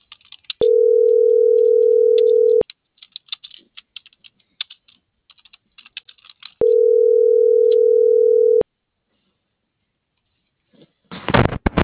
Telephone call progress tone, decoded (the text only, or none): ringback tone